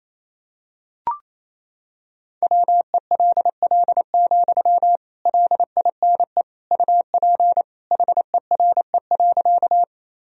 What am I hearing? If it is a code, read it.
Morse code, heard as a keyed tone, about 28 wpm, 700 Hz. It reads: WELL, LINE UP HERE.
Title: Sentences from Top 1000 Words - Review Part 1 - 28wpm